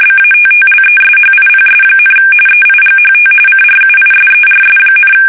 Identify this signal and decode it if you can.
RTTY, about 45.45 baud. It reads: BATTLESHIPS, EIGHT CRUISERS,
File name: fsk850 / battleshipnj